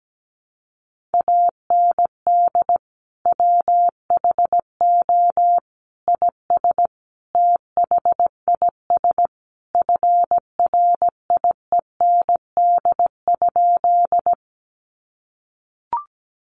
Morse code, heard as a keyed tone, about 17 wpm, 700 Hz. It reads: AND WHO IS THIS FRIEND?
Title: Sentences from Top 300 Words 17wpm